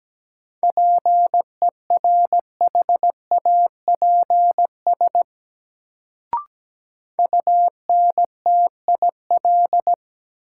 Morse code, heard as a keyed tone, about 17 wpm, 700 Hz. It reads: PERHAPS UNTIL
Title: Sentences from Top 500 Words - New Word Drill - 17wpm